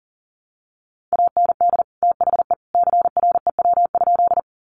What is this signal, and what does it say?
Morse code, heard as a keyed tone, about 40 wpm, 700 Hz. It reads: AND THE CREW?
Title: Sentences from Top 1500 Words 40wpm